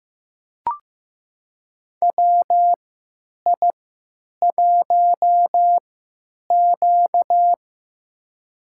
Morse code, heard as a keyed tone, about 15 wpm, spaced as for 8 wpm, 700 Hz. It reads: WI1Q
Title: USA Call Signs - 4 Characters - 3x Character Spacing - 15wpm